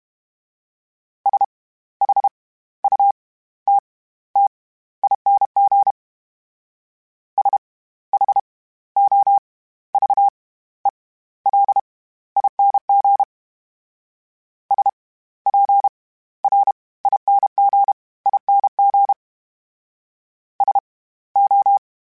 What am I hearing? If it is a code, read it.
Morse code, heard as a keyed tone, about 32 wpm, spaced as for 16 wpm, 800 Hz. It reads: SHUTTING SHOVELING SPRINGING SO
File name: Special INGs / Special INGs-mp3s / SlowToFaster ing @30